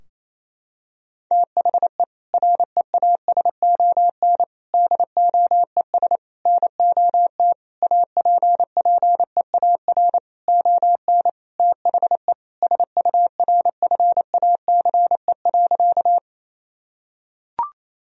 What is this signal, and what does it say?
Morse code, heard as a keyed tone, about 28 wpm, 700 Hz. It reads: THE REASON DOES NOT APPEAR ON THE SURFACE.
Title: Sentences from Top 1000 Words - Review Part 3 - 28wpm